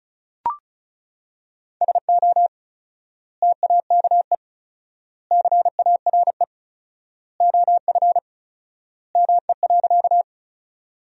Morse code, heard as a keyed tone, about 36 wpm, 700 Hz. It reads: SO TAKE CARE OF ME.
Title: 4x Word-Spacing - Sentences from Top 300 Words - 35wpm